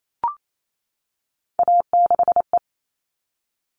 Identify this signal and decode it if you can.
Morse code, heard as a keyed tone, about 28 wpm, 700 Hz. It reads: A6E
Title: All Cut Numbers - 3-Digit 28wpm